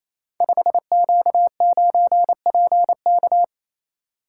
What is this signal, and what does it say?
Morse code, heard as a keyed tone, about 28 wpm, 700 Hz. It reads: 5Q9PK
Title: Code Groups - 5 Letter:Numbers - 28wpm